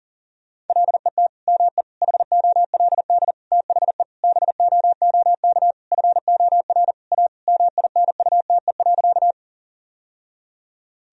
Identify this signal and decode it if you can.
Morse code, heard as a keyed tone, about 37 wpm, 700 Hz. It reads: LET ME HOLD THE BOOK FOR A MINUTE.
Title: Sentences from Top 1000 Words - Review Part 2 - 40wpm